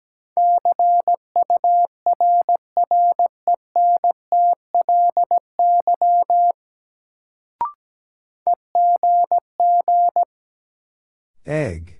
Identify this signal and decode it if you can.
Morse code, heard as a keyed tone, about 17 wpm, 700 Hz. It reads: CURRENTLY EGG
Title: Sentences from Top 1000 Words - Review Part 7 - New Word Drill - 17wpm